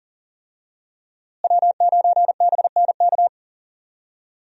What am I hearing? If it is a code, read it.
Morse code, heard as a keyed tone, about 40 wpm, 700 Hz. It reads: W9BNK
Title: USA Call Signs - 5 Characters - Repeated 2x - 40wpm